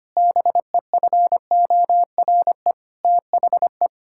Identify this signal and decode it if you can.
Morse code, heard as a keyed tone, about 25 wpm, 700 Hz. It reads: BEFORE THE
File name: Sets of 9 Words - 25wpm